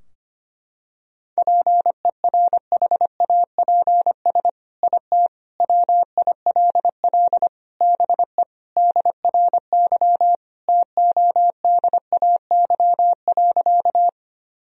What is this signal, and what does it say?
Morse code, heard as a keyed tone, about 25 wpm, 700 Hz. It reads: PERHAPS IT WILL BE DRY TODAY.
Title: Sentences from Top 1000 Words - Review Part 3 - 25wpm